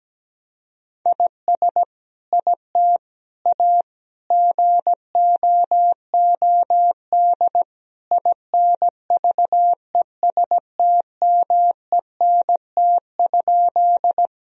Morse code, heard as a keyed tone, about 17 wpm, 700 Hz. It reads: IS IT A GOOD INVESTMENT?
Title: Sentences from Top 1000 Words - Review Part 4 - 17wpm